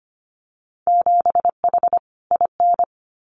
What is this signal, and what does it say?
Morse code, heard as a keyed tone, about 25 wpm, 700 Hz. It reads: 7H IN
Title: Winter Field Day Exchanges Encore 25wpm